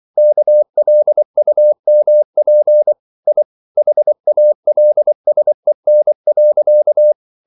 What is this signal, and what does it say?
Morse code, heard as a keyed tone, about 24 wpm, 600 Hz. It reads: KLUMP I HALSEN.
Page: Ferie 24 wpm